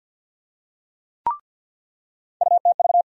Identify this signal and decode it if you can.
Morse code, heard as a keyed tone, about 46 wpm, 700 Hz. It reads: UTV
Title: USA Call Sign Postfixes - 3 Letters 50wpm